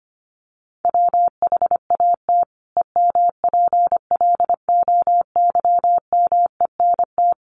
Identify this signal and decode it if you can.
Morse code, heard as a keyed tone, about 25 wpm, 700 Hz. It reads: WHAT EMPLOYMENT
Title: Sentences from Top 1500 Words 25wpm